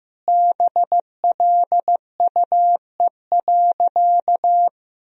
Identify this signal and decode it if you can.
Morse code, heard as a keyed tone, about 16 wpm, 700 Hz. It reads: BLUE.
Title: Sentences from Top 1000 Words - Review Part 3 - 15wpm